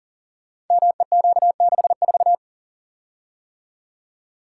Morse code, heard as a keyed tone, about 39 wpm, 700 Hz. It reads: MEQ64
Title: Code Groups - 5 Letter:Numbers - 40wpm